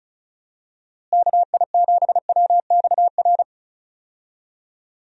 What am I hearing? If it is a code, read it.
Morse code, heard as a keyed tone, about 36 wpm, 700 Hz. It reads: KI7WXR
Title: USA Call Signs - 6 Characters - Repeated 3x - 35wpm